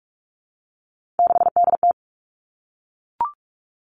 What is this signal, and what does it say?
Morse code, heard as a keyed tone, about 44 wpm, 700 Hz. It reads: 6DT
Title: All Cut Numbers - 3-Digit 45wpm